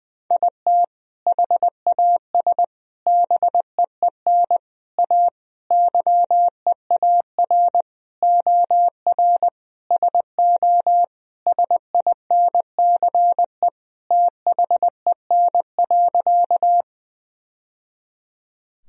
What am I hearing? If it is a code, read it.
Morse code, heard as a keyed tone, about 19 wpm, 700 Hz. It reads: IT HAS BEEN A YEAR OR SO SINCE THEN.
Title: Sentences from Top 1000 Words - Review Part 1 - 20wpm